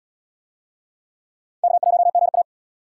Morse code, heard as a keyed tone, about 77 wpm, 700 Hz. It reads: K1GA